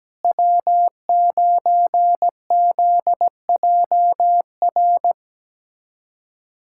Morse code, heard as a keyed tone, about 17 wpm, 700 Hz. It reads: W9ZJR
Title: USA Callsign DE Callsign - 5 Characters - 17wpm